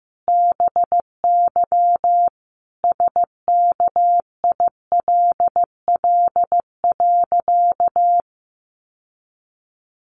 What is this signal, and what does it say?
Morse code, heard as a keyed tone, about 15 wpm, 700 Hz. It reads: BY SKILL.
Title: Sentences from Top 1200 Words 15wpm